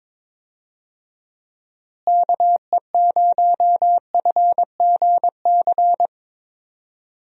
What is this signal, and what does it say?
Morse code, heard as a keyed tone, about 22 wpm, 700 Hz. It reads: KE0FGC
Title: USA Call Signs - 6 Characters - Repeated 3x - 22wpm